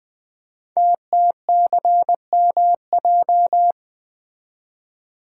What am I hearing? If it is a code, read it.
Morse code, heard as a keyed tone, about 20 wpm, 700 Hz. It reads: TTCMJ
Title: Code Groups - 5 Letters - 20wpm